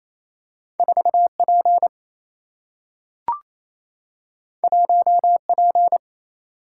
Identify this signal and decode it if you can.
Morse code, heard as a keyed tone, about 28 wpm, 700 Hz. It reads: 4 P 1 P
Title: Single-Letter Single-Number 28wpm